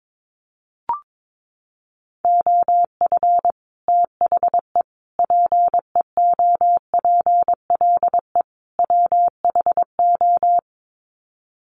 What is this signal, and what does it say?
Morse code, heard as a keyed tone, about 22 wpm, 700 Hz. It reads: OF THE PEOPLE WHO
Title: Sets of 4 Words 22wpm